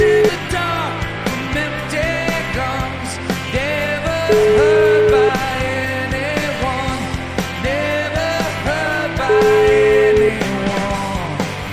Ringback tone (Europe, 425 Hz), with noise.